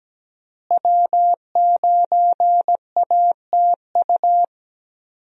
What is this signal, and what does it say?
Morse code, heard as a keyed tone, about 17 wpm, 700 Hz. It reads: W9ATU
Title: USA Call Signs - 5 Characters - Repeated 3x - 17wpm